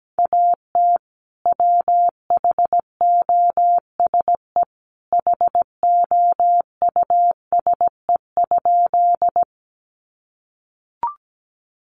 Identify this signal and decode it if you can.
Morse code, heard as a keyed tone, about 17 wpm, 700 Hz. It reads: AT WHOSE HOUSE?
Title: Sentences from Top 1200 Words 17wpm